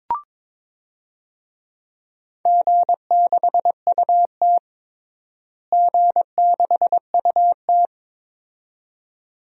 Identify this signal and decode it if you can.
Morse code, heard as a keyed tone, about 22 wpm, 700 Hz. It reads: G6UT G6UT
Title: DXCC-100 Call Signs - 4 Characters - Repeated 2x - 22wpm